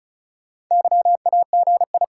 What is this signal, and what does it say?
Morse code, heard as a keyed tone, about 48 wpm, spaced as for 42 wpm, 700 Hz. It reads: YAGI
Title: Speed-Racing Top 180 QSO Elements 35wpm